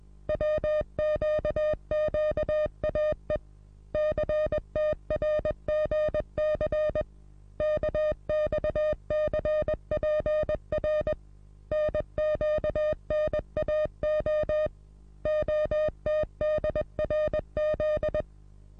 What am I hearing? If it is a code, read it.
Morse code, heard as a keyed tone, about 21 wpm, 610 Hz. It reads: WQQAE CTRGC KXCPR NQNAO OTDRZ
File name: repos / Morsekurs / DJ4UF Morsen26